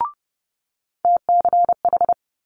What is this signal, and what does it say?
Morse code, heard as a keyed tone, about 30 wpm, 700 Hz. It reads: TCH
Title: Multi-letter Phonemes Quick Fire 30wpm